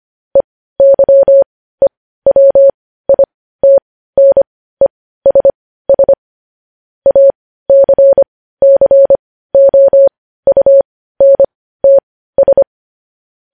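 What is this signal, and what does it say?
Morse code, heard as a keyed tone, about 25 wpm, spaced as for 15 wpm, 550 Hz. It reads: EYEWITNESS ACCOUNTS
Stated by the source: Morse code news-headline practice sent at 15 words per minute.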